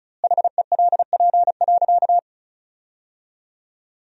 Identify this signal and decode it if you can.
Morse code, heard as a keyed tone, about 36 wpm, 700 Hz. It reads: HELP.
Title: 2x Word-Spacing - Sentences from Top 200 Words - 35wpm